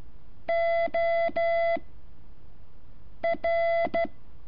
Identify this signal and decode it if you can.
Morse code, heard as a keyed tone, about 13 wpm, 670 Hz. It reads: O R
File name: curso / ejercicio 12